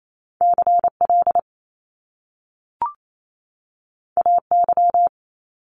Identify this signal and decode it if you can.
Morse code, heard as a keyed tone, about 28 wpm, 700 Hz. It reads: CL AY
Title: Top Two Letter Combinations in Words 28wpm